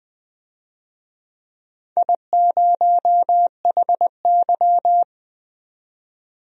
Morse code, heard as a keyed tone, about 20 wpm, 700 Hz. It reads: I0HY
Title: DXCC-100 Call Signs - 4 Characters - 20wpm